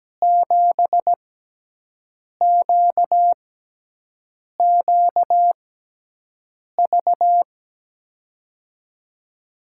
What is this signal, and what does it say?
Morse code, heard as a keyed tone, about 17 wpm, spaced as for 6 wpm, 700 Hz. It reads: 7QQV
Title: USA Call Signs - 5 Characters - 6x Character Spacing - 17wpm